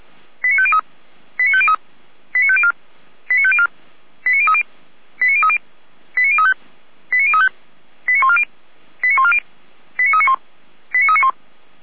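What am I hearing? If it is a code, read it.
Five-tone selective call (ZVEI), heard as quick runs of five tones, tones of about 70 ms. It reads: 89503 89503 89504 89504 89030 89030 89035 89035 89140 89140 89391 89391
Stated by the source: ZVEI five-tone call 89187 received by SDR.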